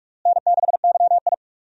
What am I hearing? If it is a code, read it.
Morse code, heard as a keyed tone, about 46 wpm, 700 Hz. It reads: N6YI
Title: USA Call Signs - 4 Characters - Repeated 2x - 45wpm